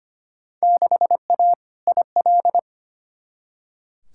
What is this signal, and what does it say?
Morse code, heard as a keyed tone, about 25 wpm, 700 Hz. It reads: 6A IL
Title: Field Day Exchanges - 25wpm